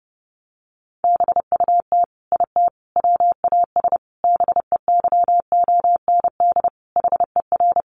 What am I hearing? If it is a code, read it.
Morse code, heard as a keyed tone, about 30 wpm, 700 Hz. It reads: BUT IT WAS BEYOND HER
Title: Sentences from Top 1200 Words 30wpm